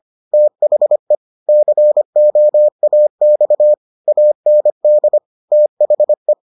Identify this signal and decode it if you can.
Morse code, heard as a keyed tone, about 25 wpm, 600 Hz. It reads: THE COAX AND THE